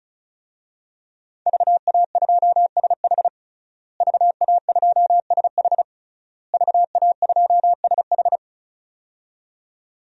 Morse code, heard as a keyed tone, about 36 wpm, 700 Hz. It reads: VA2SH VA2SH VA2SH
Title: DXCC-100 Call Signs - 5 Characters - Repeated 3x - 35wpm